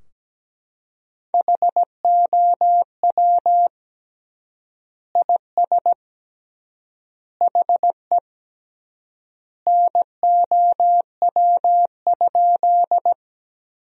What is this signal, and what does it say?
Morse code, heard as a keyed tone, about 17 wpm, 700 Hz. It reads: HOW IS HE NOW?
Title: 3x Word-Spacing - Sentences from Top 100 Words - 17wpm